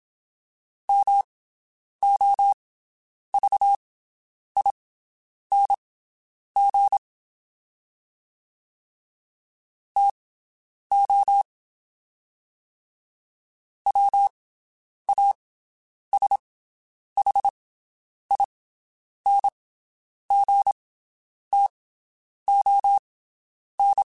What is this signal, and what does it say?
Morse code, heard as a keyed tone, about 27 wpm, spaced as for 9 wpm, 800 Hz. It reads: MOVING TO WASHINGTON
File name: (spaced) / QSOs Spaced-mp3s / WD0BGP 1 @25 sp